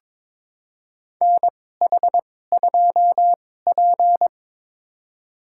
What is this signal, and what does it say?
Morse code, heard as a keyed tone, about 22 wpm, spaced as for 16 wpm, 700 Hz. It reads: NH2P
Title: USA Call Signs - 4 Characters - 2x Character Spacing - 22wpm